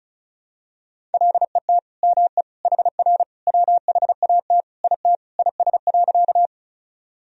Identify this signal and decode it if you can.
Morse code, heard as a keyed tone, about 36 wpm, 700 Hz. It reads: LET ME HR WHAT IT IS.
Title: Sentences from Top 300 Words and Abbreviations 35wpm